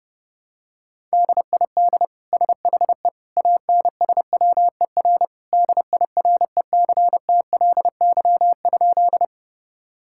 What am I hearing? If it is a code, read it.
Morse code, heard as a keyed tone, about 31 wpm, 700 Hz. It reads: DID SHE ANSWER DIRECTLY?
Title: Sentences from Top 1000 Words - Review Part 5 - 30wpm